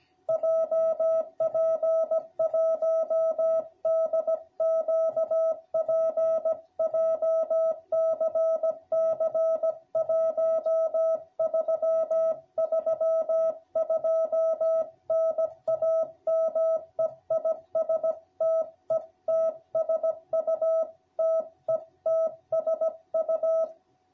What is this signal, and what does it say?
Morse code, heard as a keyed tone, about 16 wpm, 660 Hz. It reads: JP1DQP JCC 1332 NAME IS TETSU TETSU